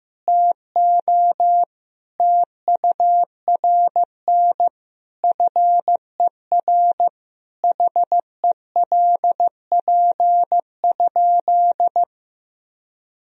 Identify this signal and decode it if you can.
Morse code, heard as a keyed tone, about 16 wpm, 700 Hz. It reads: TO TURN FER HELP?
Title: Sentences from Top 200 Words and Abbreviations 15wpm